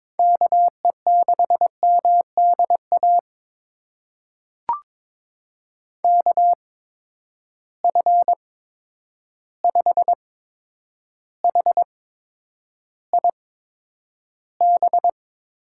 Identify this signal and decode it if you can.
Morse code, heard as a keyed tone, about 22 wpm, spaced as for 9 wpm, 700 Hz. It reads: KE6MDA KF5HIB
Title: USA Call Signs - 6 Characters - 8x Character Spacing - 22wpm